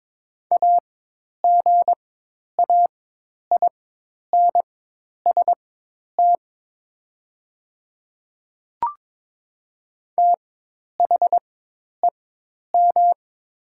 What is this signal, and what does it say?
Morse code, heard as a keyed tone, about 22 wpm, spaced as for 10 wpm, 690 Hz. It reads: AGAINST THEM
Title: 4x Character Spacing - Top 500 Words - 22wpm